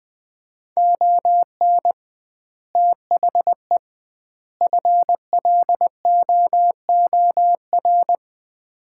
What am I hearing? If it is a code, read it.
Morse code, heard as a keyed tone, about 20 wpm, 700 Hz. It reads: ON THE FLOOR
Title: 2x Word-Spacing - Sets of 3 Words - 20wpm